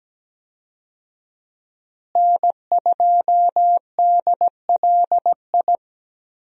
Morse code, heard as a keyed tone, about 17 wpm, 700 Hz. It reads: N2DLI
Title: USA Call Signs - 5 Characters - Repeated 2x - 17wpm